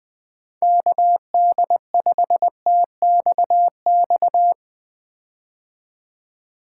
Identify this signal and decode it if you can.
Morse code, heard as a keyed tone, about 20 wpm, 700 Hz. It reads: KD5TXX
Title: K1USN SST QSO Practice - 20wpm